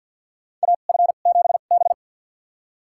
Morse code, heard as a keyed tone, about 47 wpm, spaced as for 35 wpm, 700 Hz. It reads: AF7B
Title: USA Call Signs - 4 Characters - 2x Character Spacing - 50wpm